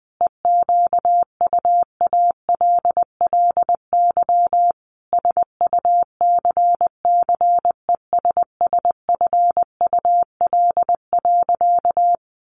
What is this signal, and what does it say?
Morse code, heard as a keyed tone, about 20 wpm, 700 Hz. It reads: EQUALLY SUCCESSFUL.
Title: Sentences from Top 1400 Words 20wpm